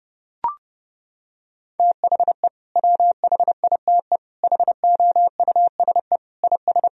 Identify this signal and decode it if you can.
Morse code, heard as a keyed tone, about 30 wpm, 700 Hz. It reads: THE WHITE HOUSE IS
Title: Sentences from Top 1000 Words - Review Part 1 - 30wpm